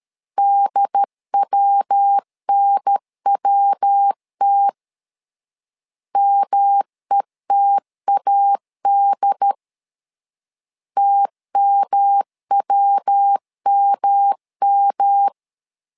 Morse code, heard as a keyed tone, about 13 wpm, 800 Hz. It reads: DWNWT METAD TMWMM
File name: LEZIONI / 3 MWRO / 3 MW 12wpm 1633